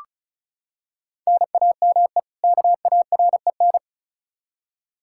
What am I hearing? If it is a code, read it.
Morse code, heard as a keyed tone, about 34 wpm, 700 Hz. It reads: NAME KAREN
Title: Practice QSO - Exchange 1 - Beginning - 35wpm